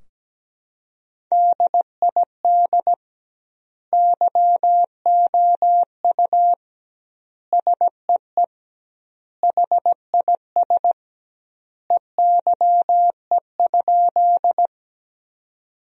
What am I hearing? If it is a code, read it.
Morse code, heard as a keyed tone, about 17 wpm, 700 Hz. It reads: DID YOU SEE HIS EYE?
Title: 2x Word-Spacing - Sentences from Top 300 Words - 17wpm